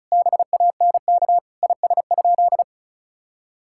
Morse code, heard as a keyed tone, about 36 wpm, 700 Hz. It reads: BANK IS?
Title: Sentences from Top 1000 Words - Review Part 6 - 35wpm